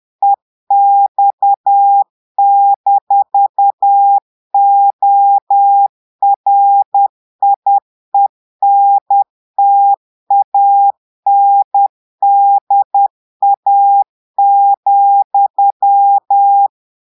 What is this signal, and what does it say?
Morse code, heard as a keyed tone, about 10 wpm, 800 Hz. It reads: EX-ORIENTANDA,
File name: Portugais - texte en clair 1 - vitesse - 10 mots minute